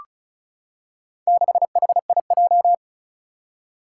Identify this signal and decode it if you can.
Morse code, heard as a keyed tone, about 36 wpm, 700 Hz. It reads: 6HIJ
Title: Four Challenging Letter-Number Combinations 35wpm